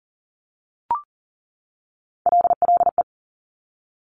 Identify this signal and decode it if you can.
Morse code, heard as a keyed tone, about 40 wpm, 700 Hz. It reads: LLE
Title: Top Three Letter Combinations in Words 40wpm